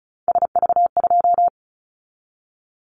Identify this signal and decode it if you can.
Morse code, heard as a keyed tone, about 36 wpm, 690 Hz. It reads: SV2
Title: Callsign Prefixes for Top 50 DXCC - 35wpm